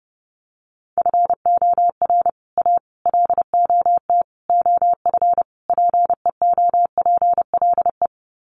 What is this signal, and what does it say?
Morse code, heard as a keyed tone, about 30 wpm, 700 Hz. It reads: FOR A LOT OF PEOPLE
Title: Sets of 5 Words Encore 30wpm